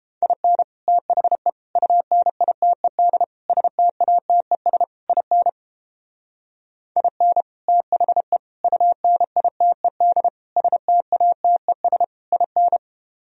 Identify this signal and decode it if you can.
Morse code, heard as a keyed tone, about 31 wpm, 700 Hz. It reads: IN THE UNITED STATES IN IN THE UNITED STATES IN